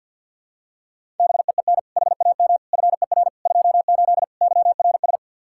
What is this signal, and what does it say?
Morse code, heard as a keyed tone, about 52 wpm, 700 Hz. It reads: BEEN HAM FER 28 YRS
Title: Practice QSO - Exchange 3 - More Info - 50wpm